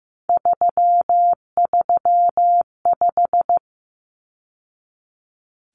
Morse code, heard as a keyed tone, about 15 wpm, 700 Hz. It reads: 335
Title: Numbers - 3-Digit 15wpm